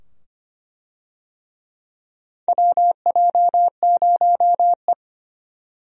Morse code, heard as a keyed tone, about 25 wpm, 700 Hz. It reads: WJ0E
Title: USA Call Signs - 4 Characters - 25wpm